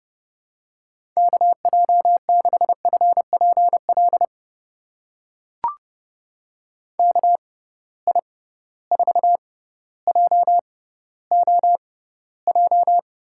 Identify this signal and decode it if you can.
Morse code, heard as a keyed tone, about 31 wpm, spaced as for 15 wpm, 700 Hz. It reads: KJ6FPL KI4JOJ
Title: USA Call Signs - 6 Characters - 6x Character Spacing - 30wpm